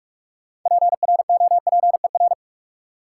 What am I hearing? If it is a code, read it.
Morse code, heard as a keyed tone, about 46 wpm, 700 Hz. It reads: PROPER
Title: Sentences from Top 800 Words - New Word Drill - 45wpm